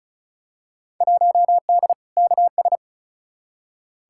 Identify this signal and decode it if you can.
Morse code, heard as a keyed tone, about 36 wpm, 700 Hz. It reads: 1D KS
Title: Field Day Exchanges - 35wpm